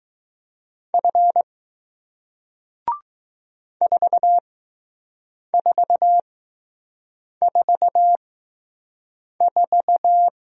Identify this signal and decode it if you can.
Morse code, heard as a keyed tone, about 20 wpm, spaced as for 6 wpm, 700 Hz. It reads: F 4444